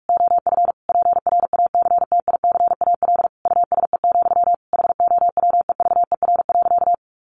Morse code, heard as a keyed tone, about 44 wpm, 700 Hz. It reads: OF PRACTICAL USE, HOWEVER.
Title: Sentences from Top 1400 Words 45wpm